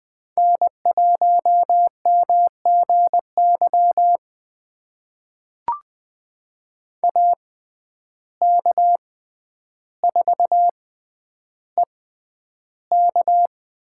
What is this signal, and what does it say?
Morse code, heard as a keyed tone, about 20 wpm, spaced as for 10 wpm, 700 Hz. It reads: N1MGY AK4EK
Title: USA Call Signs - 5 Characters - 6x Character Spacing - 20wpm